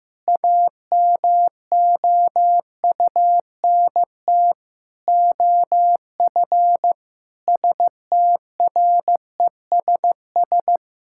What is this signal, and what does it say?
Morse code, heard as a keyed tone, about 15 wpm, 700 Hz. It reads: AMOUNT OF STRESS
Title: Sentences from Top 1000 Words - Review Part 6 - 15wpm